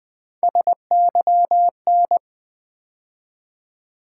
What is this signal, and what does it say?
Morse code, heard as a keyed tone, about 20 wpm, 700 Hz. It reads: SYN
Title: Build A Word - 3 Letters At A Time - 8-Letter Words - 20wpm